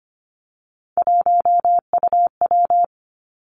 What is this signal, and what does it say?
Morse code, heard as a keyed tone, about 25 wpm, 700 Hz. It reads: 1UW
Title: Three Challenging Letter-Number Combinations 25wpm